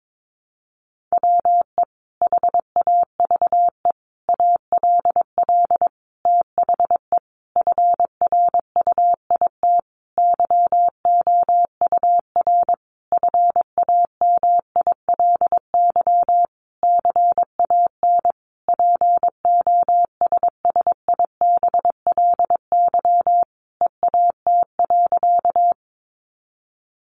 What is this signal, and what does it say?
Morse code, heard as a keyed tone, about 22 wpm, 700 Hz. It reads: WE HAVE ALL THE FRUIT YOUR FAMILY CAN POSSIBLY EAT.
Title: Sentences from Top 1300 Words 22wpm